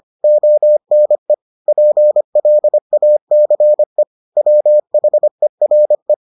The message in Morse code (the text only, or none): ONE PLACE WHERE